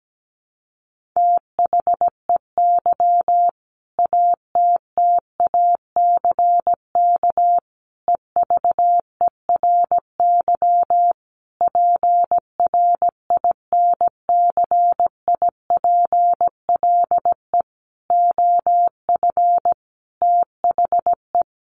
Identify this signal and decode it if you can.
Morse code, heard as a keyed tone, about 17 wpm, 700 Hz. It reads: THEY ATTACK EVERY PRINCIPLE OF THE
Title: Sentences from Top 1500 Words 17wpm